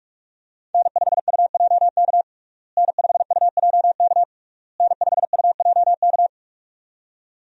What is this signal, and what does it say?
Morse code, heard as a keyed tone, about 41 wpm, 700 Hz. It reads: N5UJK N5UJK N5UJK
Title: USA Call Signs - 5 Characters - Repeated 3x - 45wpm